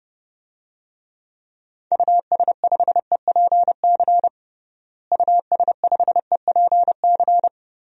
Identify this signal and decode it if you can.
Morse code, heard as a keyed tone, about 30 wpm, 700 Hz. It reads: US5EPC US5EPC
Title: DXCC-100 Call Signs - 6 Characters - Repeated 3x - 30wpm